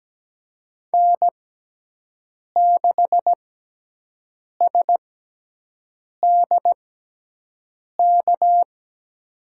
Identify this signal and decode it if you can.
Morse code, heard as a keyed tone, about 17 wpm, spaced as for 6 wpm, 700 Hz. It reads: N6SDK
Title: USA Call Signs - 6x Character Spacing - 17wpm